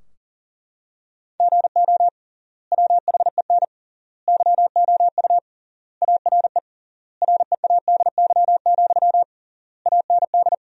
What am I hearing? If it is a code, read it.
Morse code, heard as a keyed tone, about 40 wpm, 700 Hz. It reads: GO WHEN YOU ARE READY, AND
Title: 3x Word-Spacing - Sentences from Top 500 Words - 40wpm